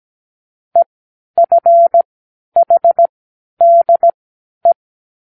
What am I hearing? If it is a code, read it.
Morse code, heard as a keyed tone, about 17 wpm, spaced as for 11 wpm, 680 Hz. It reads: EFHDE
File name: k7qo / Lesson12(A-H)